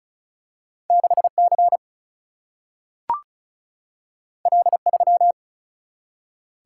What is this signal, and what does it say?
Morse code, heard as a keyed tone, about 36 wpm, 700 Hz. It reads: 6C L3
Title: Single-Letter Single-Number 35wpm